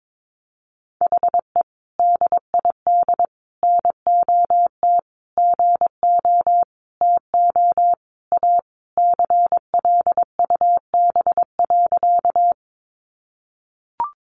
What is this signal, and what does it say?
Morse code, heard as a keyed tone, about 22 wpm, 700 Hz. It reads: HE DID NOT GO TO A CLUB.
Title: Sentences from Top 1200 Words 22wpm